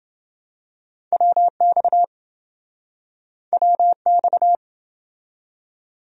Morse code, heard as a keyed tone, about 26 wpm, 700 Hz. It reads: WX WX